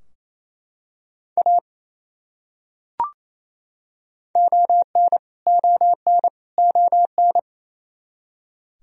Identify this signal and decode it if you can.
Morse code, heard as a keyed tone, about 28 wpm, 700 Hz. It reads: A ON ON ON
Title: Top 100 Words Repeated 28wpm